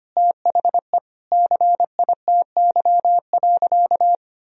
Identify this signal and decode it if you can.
Morse code, heard as a keyed tone, about 25 wpm, 700 Hz. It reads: THE CITY.
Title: Sentences from Top 1000 Words - Review Part 1 - 25wpm